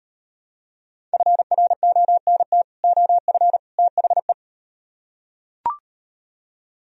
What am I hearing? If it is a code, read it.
Morse code, heard as a keyed tone, about 38 wpm, 700 Hz. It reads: FRONT OF THE